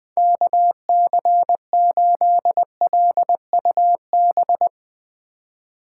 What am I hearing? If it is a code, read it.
Morse code, heard as a keyed tone, about 20 wpm, 700 Hz. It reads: KC8LUB
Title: 2023 POTA Hunter Call Signs - 20wpm